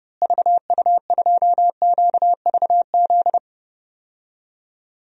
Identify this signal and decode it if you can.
Morse code, heard as a keyed tone, about 30 wpm, 700 Hz. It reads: VU2QVZ
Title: DXCC-100 Call Signs - 6 Characters - Repeated 3x - 30wpm